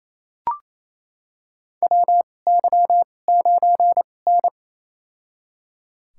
Morse code, heard as a keyed tone, about 28 wpm, spaced as for 20 wpm, 700 Hz. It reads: WY9N
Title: USA Call Signs - 4 Characters - 2x Character Spacing - 28wpm